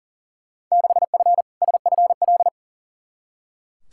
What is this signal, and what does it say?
Morse code, heard as a keyed tone, about 37 wpm, 700 Hz. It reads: 6F SFL
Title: Field Day Exchanges Encore - 40wpm